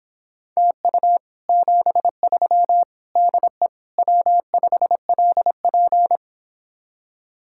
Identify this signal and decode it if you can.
Morse code, heard as a keyed tone, about 29 wpm, 700 Hz. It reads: TU 73 DE W5LP